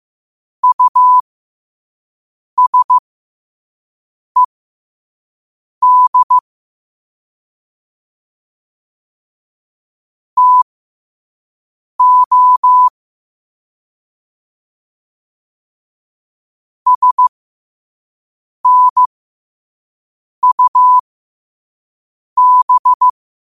Morse code, heard as a keyed tone, about 15 wpm, spaced as for 5 wpm, 1000 Hz. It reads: USED TO SNUB